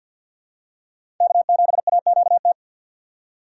Morse code, heard as a keyed tone, about 52 wpm, 700 Hz. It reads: K7AQT
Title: USA Call Signs - 5 Characters - 3x Character Spacing - 50wpm